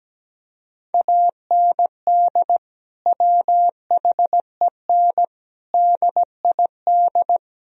Morse code, heard as a keyed tone, about 17 wpm, 700 Hz. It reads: AND WHEN DID
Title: Sentences from Top 1000 Words - Review Part 4 - 17wpm